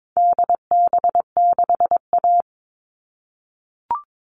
Morse code, heard as a keyed tone, about 22 wpm, 700 Hz. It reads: DB6A
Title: All Cut Numbers - 4-Digit 22wpm